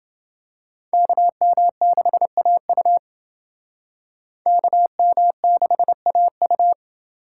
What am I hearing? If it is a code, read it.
Morse code, heard as a keyed tone, about 28 wpm, 700 Hz. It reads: KM6AU KM6AU